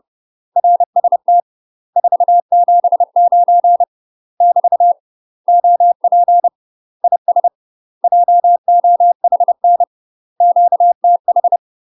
Morse code, heard as a keyed tone, about 30 wpm, 700 Hz. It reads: RST 479 <BT> OP IS JOHN QTH
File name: UK MASTER QSOs TRAINING / 30WPM / UK MASTER QSOs 0044